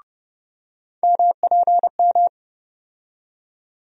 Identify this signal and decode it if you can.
Morse code, heard as a keyed tone, about 31 wpm, 700 Hz. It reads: MPM
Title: USA Call Sign Postfixes - 3 Letters 30wpm